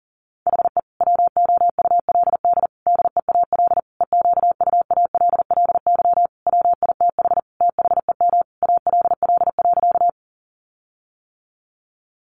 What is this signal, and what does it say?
Morse code, heard as a keyed tone, about 40 wpm, 700 Hz. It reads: HE WOULD DEAL EQUALLY WITH THEM ALL.
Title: Sentences from Top 1400 Words 40wpm